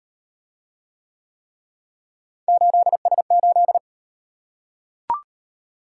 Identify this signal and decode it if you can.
Morse code, heard as a keyed tone, about 38 wpm, 700 Hz. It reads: 8S8